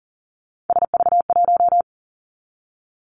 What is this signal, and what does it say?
Morse code, heard as a keyed tone, about 40 wpm, 700 Hz. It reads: SV1
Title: Callsign Prefixes for Top 50 DXCC - 40wpm